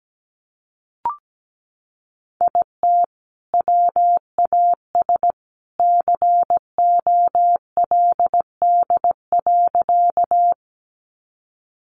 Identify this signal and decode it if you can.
Morse code, heard as a keyed tone, about 17 wpm, 700 Hz. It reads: IT WAS COLD.
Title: Sentences from Top 400 Words 17wpm